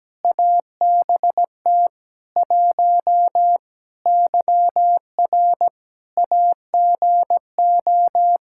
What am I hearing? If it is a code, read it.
Morse code, heard as a keyed tone, about 17 wpm, 700 Hz. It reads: ABT 1 YR AGO
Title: Practice QSO - Exchange 3 - More Info - 17wpm